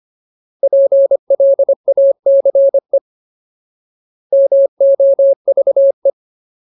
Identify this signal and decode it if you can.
Morse code, heard as a keyed tone, about 25 wpm, 550 Hz. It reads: PLACE MOVE
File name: Ultimate Word List / 25WPM Ultimate